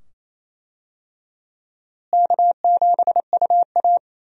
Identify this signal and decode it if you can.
Morse code, heard as a keyed tone, about 28 wpm, 700 Hz. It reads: K7UA
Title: USA Call Signs - 4 Characters - Repeated 2x - 28wpm